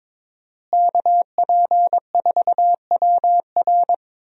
Morse code, heard as a keyed tone, about 22 wpm, 700 Hz. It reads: KP4WR
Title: USA Call Signs - 5 Characters - Repeated 2x - 22wpm